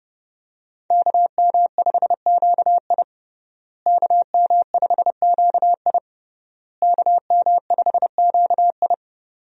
Morse code, heard as a keyed tone, about 30 wpm, 700 Hz. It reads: KM5QI KM5QI KM5QI
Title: USA Call Signs - 5 Characters - Repeated 3x - 30wpm